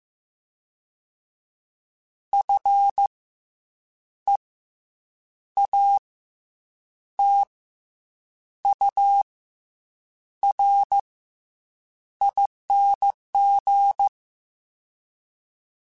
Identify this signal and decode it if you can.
Morse code, heard as a keyed tone, about 15 wpm, spaced as for 7 wpm, 800 Hz. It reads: FEATURING
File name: Special INGs-mp3s / Special INGs-mp3s / Fast ing @13